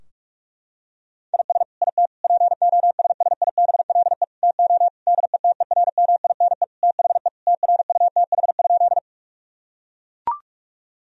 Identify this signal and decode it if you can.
Morse code, heard as a keyed tone, about 41 wpm, 700 Hz. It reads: IS IT POSSIBLE TO DETERMINE THE TRUTH?
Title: Sentences from Top 1000 Words - Review Part 4 - 45wpm